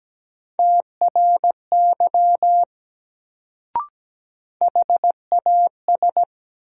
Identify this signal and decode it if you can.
Morse code, heard as a keyed tone, about 17 wpm, 700 Hz. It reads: TRY HAS
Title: Sets of 2 Words - New Word Drill - 17wpm